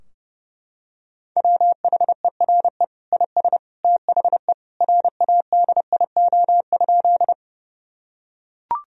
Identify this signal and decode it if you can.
Morse code, heard as a keyed tone, about 30 wpm, 700 Hz. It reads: WHERE IS THE RADIO?
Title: Sentences from Top 1000 Words - Review Part 3 - 30wpm